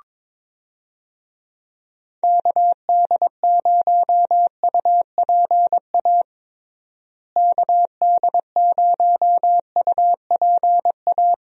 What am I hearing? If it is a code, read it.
Morse code, heard as a keyed tone, about 22 wpm, 700 Hz. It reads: KD0UPA KD0UPA
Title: USA Call Signs - 6 Characters - Repeated 2x - 22wpm